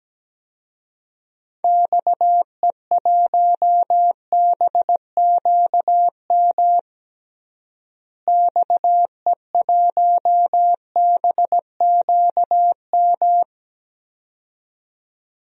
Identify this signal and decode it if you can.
Morse code, heard as a keyed tone, about 17 wpm, 700 Hz. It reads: XE1BQM XE1BQM
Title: DXCC-100 Call Signs - 6 Characters - Repeated 2x - 17wpm